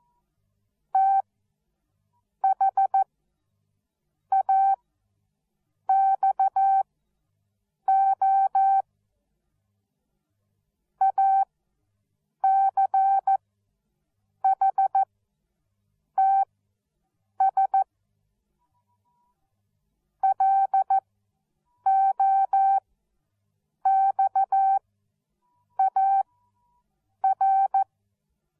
Morse code, heard as a keyed tone, about 14 wpm, spaced as for 6 wpm, 780 Hz. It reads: THAXO ACHTS LOXAR
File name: Onlinelehrgang / morsen / DJ4UF Morsen13